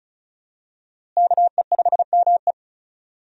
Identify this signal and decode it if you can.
Morse code, heard as a keyed tone, about 36 wpm, 700 Hz. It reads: KE5ME
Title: USA Call Signs - 5 Characters - Repeated 3x - 35wpm